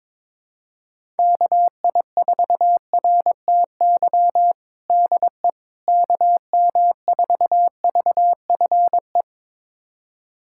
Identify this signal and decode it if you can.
Morse code, heard as a keyed tone, about 22 wpm, 700 Hz. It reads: KI4RTY DE KM4VFE
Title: USA Callsign DE Callsign - 6 Characters - 22wpm